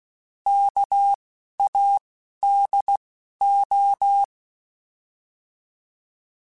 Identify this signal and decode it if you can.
Morse code, heard as a keyed tone, about 16 wpm, 800 Hz. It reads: K A D O
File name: Alphabetical / Alphabetical mp3s / Ks @10